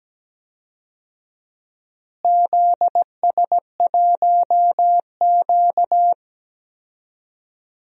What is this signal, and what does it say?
Morse code, heard as a keyed tone, about 17 wpm, 700 Hz. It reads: ZS1Q
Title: DXCC-100 Call Signs - 4 Characters - Repeated 2x - 17wpm